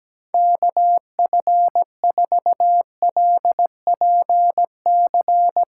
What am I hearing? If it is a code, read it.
Morse code, heard as a keyed tone, about 17 wpm, 700 Hz. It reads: KF4LPC
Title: USA Callsign DE Callsign - 6 Characters - 17wpm